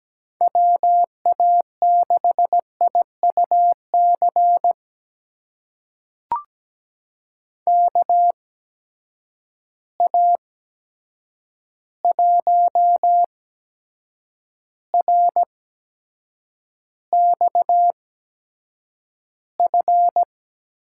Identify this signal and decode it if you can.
Morse code, heard as a keyed tone, about 17 wpm, spaced as for 7 wpm, 700 Hz. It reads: WA6IUC KA1RXF
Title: USA Call Signs - 6 Characters - 8x Character Spacing - 17wpm